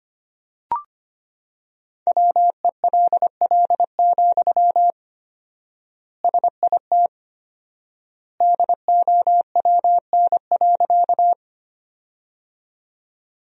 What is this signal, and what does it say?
Morse code, heard as a keyed tone, about 25 wpm, 700 Hz. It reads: WELL, SIT DOWN.
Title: 4x Word-Spacing - Sentences from Top 500 Words - 25wpm